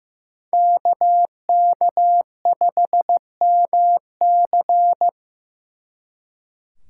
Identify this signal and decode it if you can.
Morse code, heard as a keyed tone, about 18 wpm, 700 Hz. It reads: KK5MC
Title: Speed Racing - USA 2x2 Call Signs 15wpm